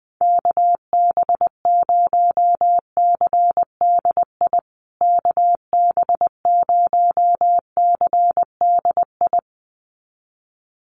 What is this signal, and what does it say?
Morse code, heard as a keyed tone, about 20 wpm, 700 Hz. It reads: KB0CDI KB0CDI
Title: USA Call Signs Repeated 2x 20wpm